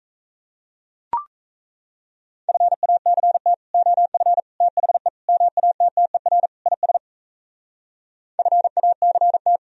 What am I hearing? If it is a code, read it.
Morse code, heard as a keyed tone, about 35 wpm, 700 Hz. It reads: FACT OF THE MATTER IS FACT
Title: Speed-Racing - Sets of 5 Words 28wpm